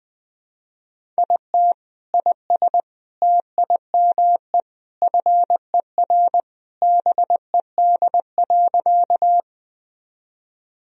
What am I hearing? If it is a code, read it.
Morse code, heard as a keyed tone, about 21 wpm, 700 Hz. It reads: IT IS TIME FER BED.
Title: Sentences from Top 500 Words and Abbreviations 20wpm